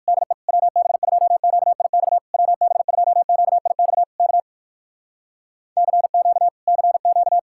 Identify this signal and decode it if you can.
Morse code, heard as a keyed tone, about 69 wpm, spaced as for 60 wpm, 700 Hz. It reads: DE WB2QIX WB2QIX K CQ CQ